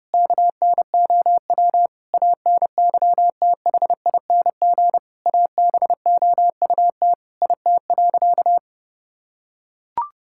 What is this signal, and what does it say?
Morse code, heard as a keyed tone, about 30 wpm, 700 Hz. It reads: KNOW ANYTHING ABOUT IT.
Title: Sentences from Top 1600 Words 30wpm